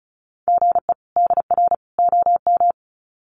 Morse code, heard as a keyed tone, about 34 wpm, 700 Hz. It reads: GE DR OM
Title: Common QSO Phrases 35wpm